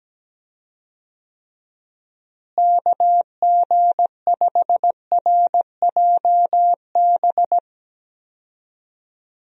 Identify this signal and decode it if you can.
Morse code, heard as a keyed tone, about 17 wpm, 700 Hz. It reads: KG5RJB
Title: USA Call Signs - 6 Characters - Repeated 3x - 17wpm